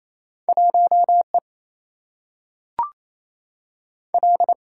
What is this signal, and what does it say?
Morse code, heard as a keyed tone, about 28 wpm, 700 Hz. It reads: 1 E L
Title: Single-Letter Single-Number 28wpm